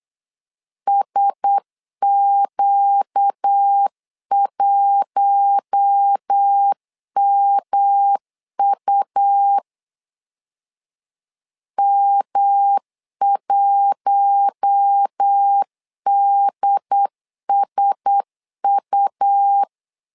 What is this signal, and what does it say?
Morse code, heard as a keyed tone, about 8 wpm, 800 Hz. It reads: SQ1MU M1DSU
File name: LEZIONI / 4 Q1S 8wpm 2710